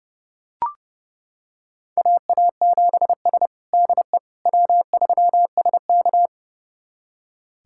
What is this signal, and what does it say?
Morse code, heard as a keyed tone, about 30 wpm, 700 Hz. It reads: AA7S DE W3SK
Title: USA Callsign DE Callsign - 4 Characters - 30wpm